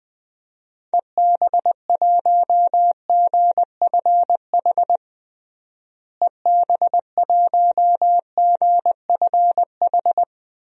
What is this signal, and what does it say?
Morse code, heard as a keyed tone, about 21 wpm, 700 Hz. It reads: EB1GFH EB1GFH
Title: DXCC-100 Call Signs - 6 Characters - Repeated 3x - 20wpm